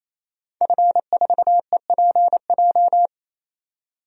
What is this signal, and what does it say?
Morse code, heard as a keyed tone, about 28 wpm, 700 Hz. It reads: F4EPJ
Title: DXCC-100 Call Signs - 5 Characters - 28wpm